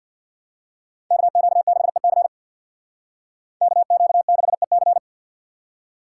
Morse code, heard as a keyed tone, about 52 wpm, 700 Hz. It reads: KQ6EC KQ6EC
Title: Speed Racing - USA 2x2 Call Signs 50wpm